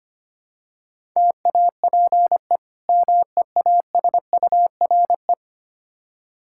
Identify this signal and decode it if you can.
Morse code, heard as a keyed tone, about 25 wpm, 700 Hz. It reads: TAPE MEASURE
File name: practice / 25wpm / Difficult Words with Hints - 25wpm